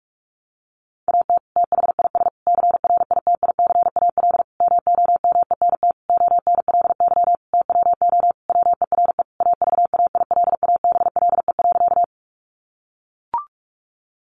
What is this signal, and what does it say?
Morse code, heard as a keyed tone, about 44 wpm, 700 Hz. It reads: AT THIS CRITICAL MOMENT ONLY TWO WERE AVAILABLE.
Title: Sentences from Top 1100 Words 45wpm